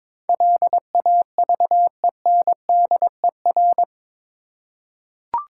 Morse code, heard as a keyed tone, about 22 wpm, 690 Hz. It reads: LAVENDER
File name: Difficult Words with Hints - 22wpm